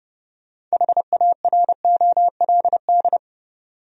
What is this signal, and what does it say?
Morse code, heard as a keyed tone, about 30 wpm, 700 Hz. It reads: HAROLD
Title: USA Common Names - 30wpm